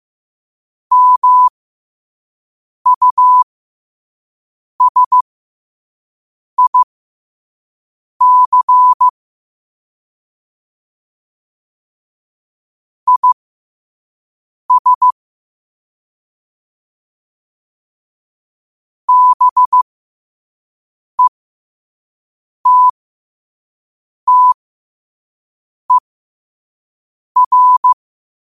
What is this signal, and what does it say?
Morse code, heard as a keyed tone, about 15 wpm, spaced as for 5 wpm, 1000 Hz. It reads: MUSIC IS BETTER
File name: qotdfiles / 5wpm